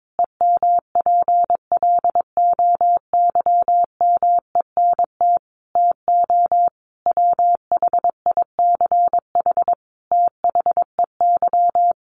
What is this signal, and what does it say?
Morse code, heard as a keyed tone, about 22 wpm, 700 Hz. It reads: EMPLOYMENT TO WHICH THEY
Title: Sentences from Top 1500 Words 22wpm